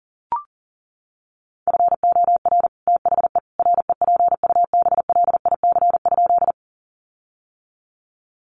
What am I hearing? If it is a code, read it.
Morse code, heard as a keyed tone, about 40 wpm, 700 Hz. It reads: FOR THE REPUBLIC?
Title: Sentences from Top 1500 Words 40wpm